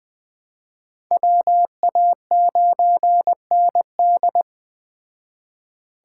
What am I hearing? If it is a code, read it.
Morse code, heard as a keyed tone, about 21 wpm, 700 Hz. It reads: WA9ND
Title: Speed Racing - USA 2x2 Call Signs 17wpm